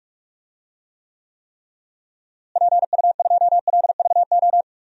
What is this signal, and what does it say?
Morse code, heard as a keyed tone, about 46 wpm, 700 Hz. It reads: PU2LVO
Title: DXCC-100 Call Signs - 6 Characters - Repeated 3x - 45wpm